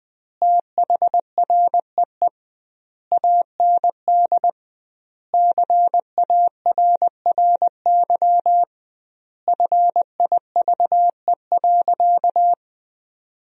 Morse code, heard as a keyed tone, about 20 wpm, 700 Hz. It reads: THREE AND CARRY FIVE.
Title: 2x Word-Spacing - Sentences from Top 400 Words - 20wpm